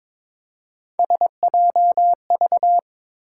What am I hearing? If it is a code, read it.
Morse code, heard as a keyed tone, about 22 wpm, 700 Hz. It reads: SJV
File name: ARRL Sections v2 - 22wpm